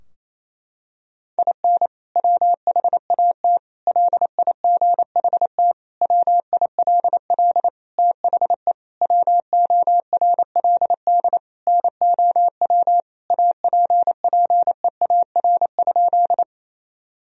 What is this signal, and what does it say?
Morse code, heard as a keyed tone, about 28 wpm, 700 Hz. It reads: IN WHAT LIGHT WILL THE WORLD NOW APPEAR?
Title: Sentences from Top 1000 Words - Review Part 2 - 28wpm